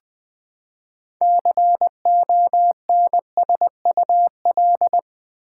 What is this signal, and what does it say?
Morse code, heard as a keyed tone, about 20 wpm, 700 Hz. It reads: CONSUL
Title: Build A Word - 6 Letters At A Time - 12-Letter Words - 20wpm